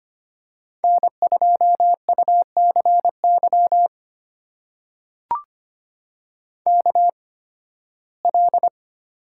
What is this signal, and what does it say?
Morse code, heard as a keyed tone, about 25 wpm, spaced as for 16 wpm, 700 Hz. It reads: N2UCY KL
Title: USA Call Signs - 5 Characters - 8x Character Spacing - 25wpm